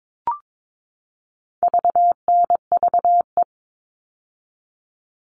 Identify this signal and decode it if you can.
Morse code, heard as a keyed tone, about 22 wpm, 700 Hz. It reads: VNVE
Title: All Cut Numbers - 4-Digit 22wpm